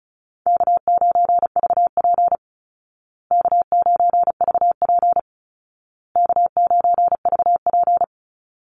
Morse code, heard as a keyed tone, about 34 wpm, 700 Hz. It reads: K9VP K9VP K9VP
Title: Call Signs Repeated 2x - 35wpm